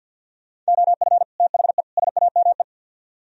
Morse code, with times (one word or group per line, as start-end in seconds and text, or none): 0.7-1.2 OF
1.4-1.8 THE
2.0-2.6 SAME